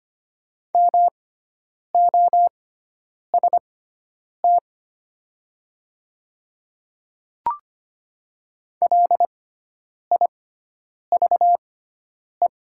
Morse code, heard as a keyed tone, about 25 wpm, spaced as for 9 wpm, 700 Hz. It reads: MOST LIVE
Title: 6x Character Spacing - Top 200 Words - 25wpm